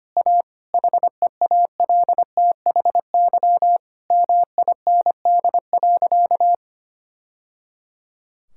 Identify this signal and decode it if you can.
Morse code, heard as a keyed tone, about 25 wpm, 700 Hz. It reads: A HEALTHY MIND.
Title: Sentences from Top 1000 Words - Review Part 5 - 25wpm